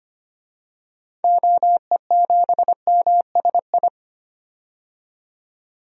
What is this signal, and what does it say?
Morse code, heard as a keyed tone, about 25 wpm, 700 Hz. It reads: OE7MSI
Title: DXCC-100 Call Signs - 6 Characters - 25wpm